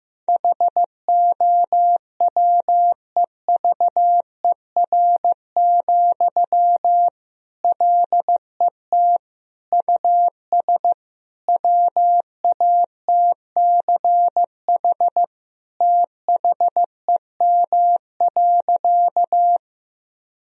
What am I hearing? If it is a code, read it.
Morse code, heard as a keyed tone, about 15 wpm, 700 Hz. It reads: HOWEVER, LET US WATCH THEM.
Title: Sentences from Top 1000 Words - Review Part 1 - 15wpm